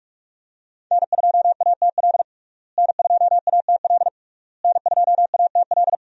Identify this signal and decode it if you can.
Morse code, heard as a keyed tone, about 46 wpm, 700 Hz. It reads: N2ATL N2ATL N2ATL
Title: USA Call Signs - 5 Characters - Repeated 3x - 45wpm